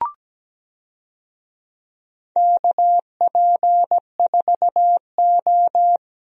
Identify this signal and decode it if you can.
Morse code, heard as a keyed tone, about 17 wpm, 700 Hz. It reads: KP4O
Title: DXCC-100 Call Signs - 4 Characters - 17wpm